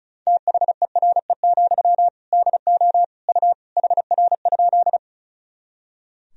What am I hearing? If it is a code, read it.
Morse code, heard as a keyed tone, about 36 wpm, 700 Hz. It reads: THERE, DO U HR?
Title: Sentences from Top 300 Words and Abbreviations 35wpm